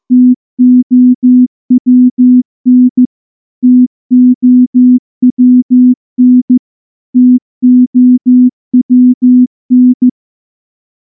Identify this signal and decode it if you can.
Morse code, heard as a keyed tone, about 14 wpm, 250 Hz. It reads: TOWN TOWN TOWN
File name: morse / morse-training-15WPM